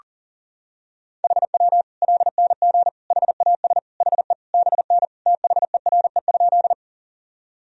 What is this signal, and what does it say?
Morse code, heard as a keyed tone, about 40 wpm, 700 Hz. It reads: HW LNG HAS HE BN THERE?
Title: Sentences from Top 100 Words and Abbreviations 40wpm